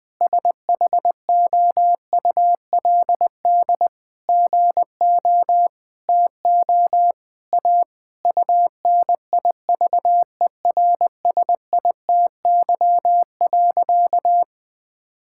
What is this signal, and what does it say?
Morse code, heard as a keyed tone, about 20 wpm, 700 Hz. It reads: SHOULD GO TO A UNIVERSITY.
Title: Sentences from Top 1000 Words - Review Part 5 - 20wpm